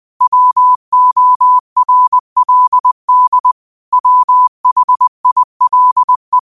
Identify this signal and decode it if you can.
Morse code, heard as a keyed tone, about 20 wpm, 1000 Hz. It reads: WORLD WHILE
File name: qotdfiles / 20wpm